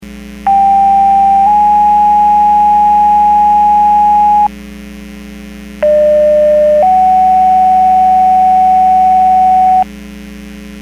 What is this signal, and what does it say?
Two-tone page, heard as a one-second tone, then a three-second tone, about 790 then 830 Hz, about 600 then 750 Hz. It is heard from 0.5 to 9.8 s.